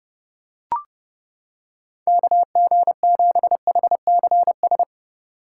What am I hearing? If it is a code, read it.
Morse code, heard as a keyed tone, about 31 wpm, 700 Hz. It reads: KG7HCS